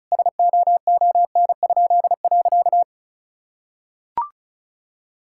Morse code, heard as a keyed tone, about 34 wpm, 700 Hz. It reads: SOON?.
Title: Pangram Challenge v2 - 35wpm